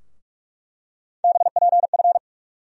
Morse code, heard as a keyed tone, about 46 wpm, 700 Hz. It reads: BPF
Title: USA Call Sign Postfixes - 3 Letters 45wpm